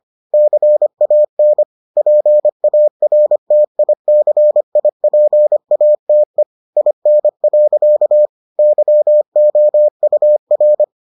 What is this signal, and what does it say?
Morse code, heard as a keyed tone, about 25 wpm, 600 Hz. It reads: CAN PARTICIPATE IN. YOUR